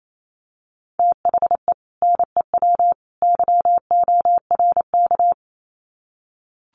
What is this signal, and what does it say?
Morse code, heard as a keyed tone, about 28 wpm, 700 Hz. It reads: THE NEW YORK
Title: Sets of 3 Words 28wpm